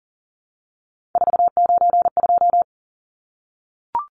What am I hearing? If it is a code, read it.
Morse code, heard as a keyed tone, about 40 wpm, 700 Hz. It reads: 492
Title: Numbers - 3-Digit 40wpm